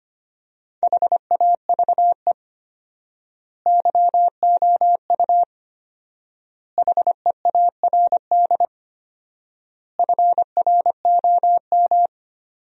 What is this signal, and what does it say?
Morse code, heard as a keyed tone, about 25 wpm, 700 Hz. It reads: HAVE YOU HEARD FROM
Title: 4x Word-Spacing - Sentences from Top 400 Words - 25wpm